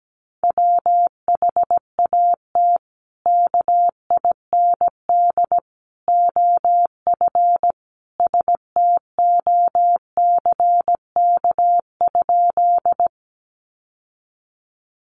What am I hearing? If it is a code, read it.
Morse code, heard as a keyed tone, about 17 wpm, 700 Hz. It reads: WHAT KIND OF STOCK?
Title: Sentences from Top 800 Words 17wpm